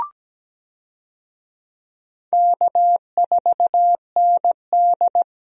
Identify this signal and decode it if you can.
Morse code, heard as a keyed tone, about 17 wpm, 700 Hz. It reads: K4ND
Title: USA Call Signs - 4 Characters - 17wpm